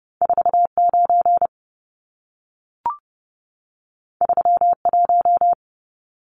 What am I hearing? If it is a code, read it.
Morse code, heard as a keyed tone, about 30 wpm, 700 Hz. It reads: 49 31
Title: Numbers - 2-Digit 30wpm